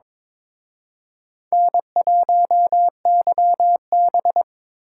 Morse code, heard as a keyed tone, about 26 wpm, 700 Hz. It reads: N1YB
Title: Speed Racing - USA Call Signs 22wpm